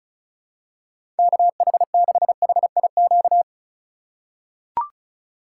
Morse code, heard as a keyed tone, about 34 wpm, 700 Hz. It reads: KH6HIQ
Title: USA Call Signs - 6 Characters - 2x Character Spacing - 35wpm